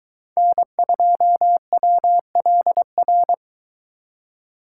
Morse code, heard as a keyed tone, about 22 wpm, 700 Hz. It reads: N2WLR